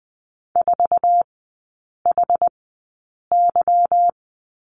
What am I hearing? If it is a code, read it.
Morse code, heard as a keyed tone, about 20 wpm, spaced as for 8 wpm, 690 Hz. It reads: 4HY
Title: 159 Lesson 26e - Letter g - Three Characters 20wpm